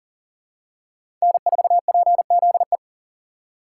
Morse code, heard as a keyed tone, about 40 wpm, 700 Hz. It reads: N4PZE
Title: USA Call Signs - 5 Characters - 2x Character Spacing - 40wpm